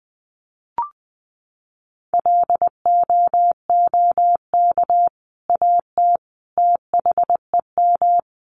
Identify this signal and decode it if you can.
Morse code, heard as a keyed tone, about 20 wpm, 700 Hz. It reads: LOOK AT THEM
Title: Sentences from Top 1300 Words 20wpm